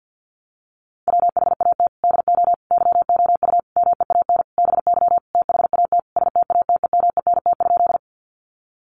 Morse code, heard as a keyed tone, about 50 wpm, 700 Hz. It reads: WHAT DO YOU MEAN BY THAT STATEMENT?
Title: Sentences from Top 1200 Words 50wpm